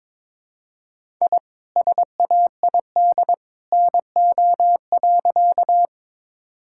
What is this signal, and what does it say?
Morse code, heard as a keyed tone, about 22 wpm, 700 Hz. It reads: I SAID NO.
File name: practice / 22wpm / Sentences with 2-Word Phrases - 22wpm